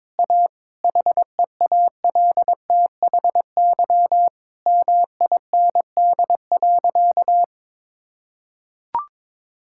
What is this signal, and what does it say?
Morse code, heard as a keyed tone, about 22 wpm, 700 Hz. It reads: A HEALTHY MIND.
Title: Sentences from Top 1000 Words - Review Part 5 - 22wpm